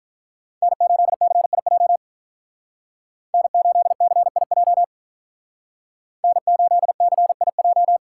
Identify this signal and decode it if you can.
Morse code, heard as a keyed tone, about 42 wpm, 700 Hz. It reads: N8CIJ N8CIJ N8CIJ
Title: Speed Racing - USA 1x3 Call Signs 35wpm